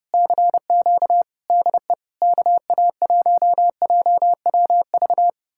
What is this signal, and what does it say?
Morse code, heard as a keyed tone, about 34 wpm, 700 Hz. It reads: CQ DE KA1JWV